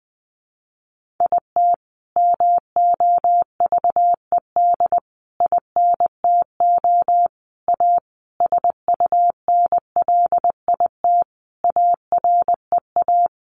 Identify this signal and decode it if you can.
Morse code, heard as a keyed tone, about 20 wpm, 700 Hz. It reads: IT MOVED INTO A SUNLIT AREA
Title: A Sound Of Thunder 20wpm